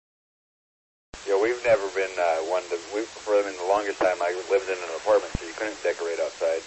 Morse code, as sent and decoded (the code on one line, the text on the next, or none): none